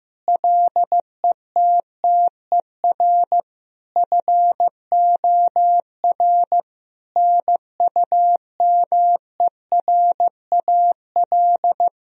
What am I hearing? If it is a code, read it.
Morse code, heard as a keyed tone, about 15 wpm, 700 Hz. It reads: LETTER FOR NUMERAL
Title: Sentences from Top 1000 Words - Review Part 2 - 15wpm